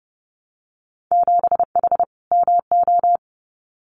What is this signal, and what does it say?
Morse code, heard as a keyed tone, about 31 wpm, 700 Hz. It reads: 7H MO
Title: Winter Field Day Exchanges Encore 30wpm